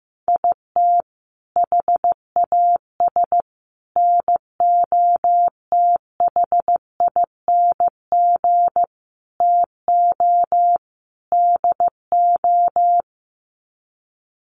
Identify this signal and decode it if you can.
Morse code, heard as a keyed tone, about 15 wpm, 700 Hz. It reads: IT HAS NOTHING TO DO
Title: Sets of 5 Words Encore 15wpm